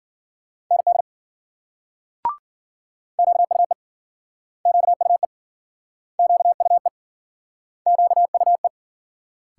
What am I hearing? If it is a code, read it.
Morse code, heard as a keyed tone, about 52 wpm, 700 Hz. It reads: ND QUE QUE QUE QUE